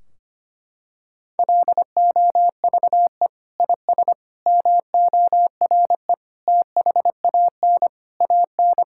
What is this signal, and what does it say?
Morse code, heard as a keyed tone, about 25 wpm, 700 Hz. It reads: LOVE IS MORE THAN AN
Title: Sentences from Top 1600 Words 25wpm